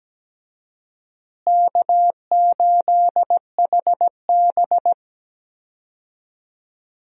Morse code, heard as a keyed tone, about 17 wpm, 700 Hz. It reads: K8HB
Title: USA Call Signs - 4 Characters - Repeated 2x - 17wpm